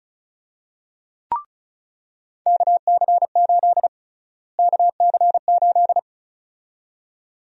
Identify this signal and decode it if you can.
Morse code, heard as a keyed tone, about 34 wpm, 700 Hz. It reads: KC8 KC8
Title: USA Call Sign Prefixes - Repeated 2x - 35wpm